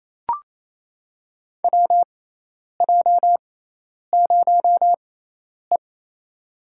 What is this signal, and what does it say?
Morse code, heard as a keyed tone, about 28 wpm, spaced as for 10 wpm, 700 Hz. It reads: WJ0E
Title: USA Call Signs - 4 Characters - 6x Character Spacing - 28wpm